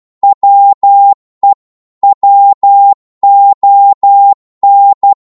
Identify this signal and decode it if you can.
Morse code, heard as a keyed tone, about 12 wpm, 800 Hz. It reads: WE WON
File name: Morse / psg-12wpm